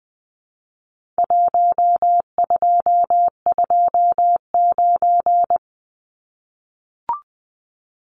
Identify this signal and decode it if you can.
Morse code, heard as a keyed tone, about 20 wpm, 700 Hz. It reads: 1229
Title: Numbers - Contest Style - 4-Digit 20wpm